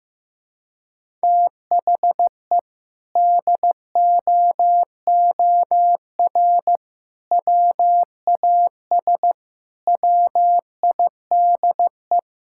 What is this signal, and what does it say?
Morse code, heard as a keyed tone, about 15 wpm, 700 Hz. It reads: THE DOOR WAS WIDE
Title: Sentences from Top 1000 Words - Review Part 6 - 15wpm